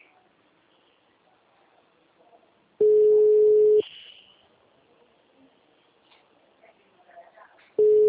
Ringback tone (Europe, 425 Hz).
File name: out- -2005- - - .28824